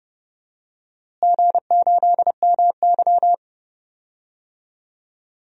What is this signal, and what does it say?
Morse code, heard as a keyed tone, about 30 wpm, 700 Hz. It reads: G8MY
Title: DXCC-100 Call Signs - 4 Characters - Repeated 3x - 30wpm